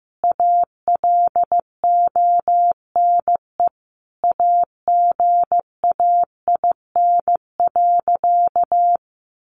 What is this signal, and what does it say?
Morse code, heard as a keyed tone, about 15 wpm, 700 Hz. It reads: ALONE AGAIN.
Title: Sentences from Top 1000 Words 15wpm